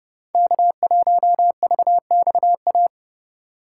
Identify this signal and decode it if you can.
Morse code, heard as a keyed tone, about 31 wpm, 700 Hz. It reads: K1VXA